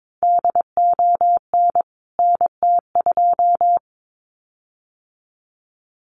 Morse code, heard as a keyed tone, about 22 wpm, 700 Hz. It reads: DON NT2
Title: CWT Exchange 22wpm